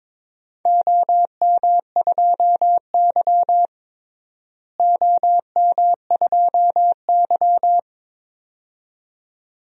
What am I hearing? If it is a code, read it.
Morse code, heard as a keyed tone, about 22 wpm, 700 Hz. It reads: OM2Y OM2Y
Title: DXCC-100 Call Signs - 4 Characters - Repeated 2x - 22wpm